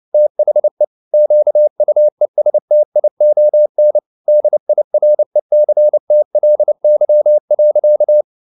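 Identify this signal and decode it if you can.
Morse code, heard as a keyed tone, about 29 wpm, 600 Hz. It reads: THE QUESTION DIRECTLY.
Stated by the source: Morse code at 29 wpm, sending Ken Follett's novel Never in English, from CW med Gnister.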